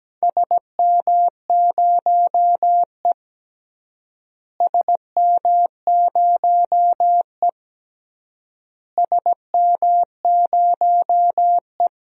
Morse code, heard as a keyed tone, about 17 wpm, 700 Hz. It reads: SM0E SM0E SM0E
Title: DXCC-100 Call Signs - 4 Characters - Repeated 3x - 17wpm